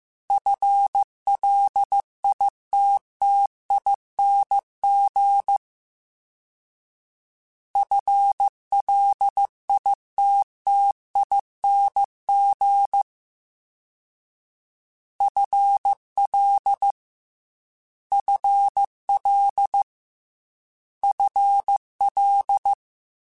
Morse code, heard as a keyed tone, about 15 wpm, 800 Hz. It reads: FLITTING FLITTING FL FL FL
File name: FL / FL mp3s / FL Prefixes 3x2 @13